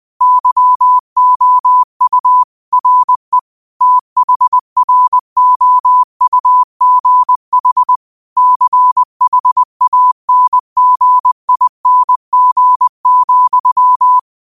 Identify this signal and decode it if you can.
Morse code, heard as a keyed tone, about 20 wpm, 1000 Hz. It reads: YOU RE THROUGH CHANGING,
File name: 20wpm